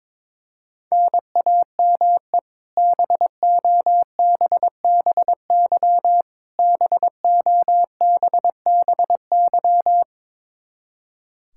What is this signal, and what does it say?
Morse code, heard as a keyed tone, about 22 wpm, 700 Hz. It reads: NAME BOBBY BOBBY
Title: Practice QSO - Exchange 1 - Beginning - 22wpm